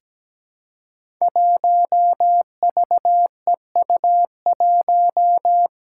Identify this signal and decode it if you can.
Morse code, heard as a keyed tone, about 17 wpm, 700 Hz. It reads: 1VEU1
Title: Five Challenging Letter-Number Combinations 17wpm